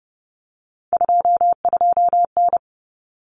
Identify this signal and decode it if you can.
Morse code, heard as a keyed tone, about 30 wpm, 700 Hz. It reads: 22N
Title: Common Cut Numbers - 3-Digit 30wpm